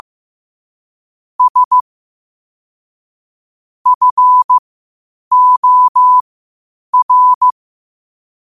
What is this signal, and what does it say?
Morse code, heard as a keyed tone, about 15 wpm, spaced as for 8 wpm, 1000 Hz. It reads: S FOR
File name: qotdfiles / 8wpm